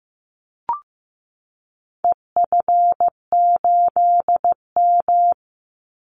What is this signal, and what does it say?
Morse code, heard as a keyed tone, about 15 wpm, 700 Hz. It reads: EF8M
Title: Call Signs Repeated 3x - 15wpm